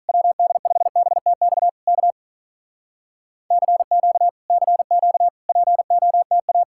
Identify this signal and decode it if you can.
Morse code, heard as a keyed tone, about 46 wpm, 700 Hz. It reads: WD5BTX K CQ CQ POTA